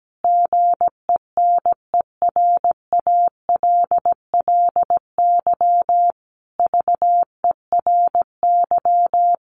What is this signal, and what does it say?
Morse code, heard as a keyed tone, about 17 wpm, 700 Hz. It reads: GENERALLY VERY
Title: Sentences from Top 1500 Words 17wpm